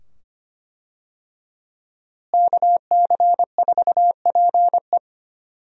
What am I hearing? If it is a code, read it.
Morse code, heard as a keyed tone, about 25 wpm, 700 Hz. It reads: KC4PE
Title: USA Call Signs - 5 Characters - Repeated 2x - 25wpm